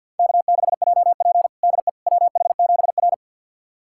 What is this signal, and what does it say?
Morse code, heard as a keyed tone, about 50 wpm, 700 Hz. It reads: K6JP DE WH7R
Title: USA Callsign DE Callsign - 4 Characters - 50wpm